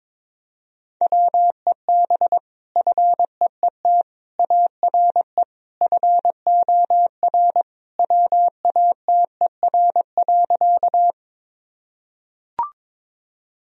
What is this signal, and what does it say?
Morse code, heard as a keyed tone, about 22 wpm, 700 Hz. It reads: WEB FEET ARE FOR WATER.
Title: Sentences from Top 1000 Words - Review Part 4 - 22wpm